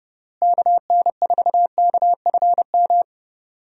Morse code, heard as a keyed tone, about 29 wpm, 700 Hz. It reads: KN4KFM
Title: USA Call Signs - 6 Characters - Repeated 3x - 30wpm